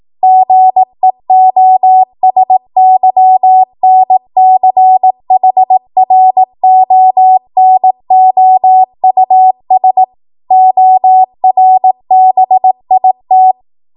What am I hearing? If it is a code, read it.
Morse code, heard as a keyed tone, about 18 wpm, 750 Hz. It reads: GEOSYNCHRONOUS ORBIT
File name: Morse / 18WPM